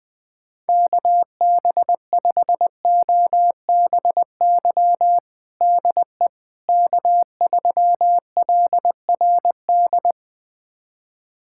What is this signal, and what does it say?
Morse code, heard as a keyed tone, about 20 wpm, 700 Hz. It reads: KB5OBY DE K3LRD
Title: USA Callsign DE Callsign 20wpm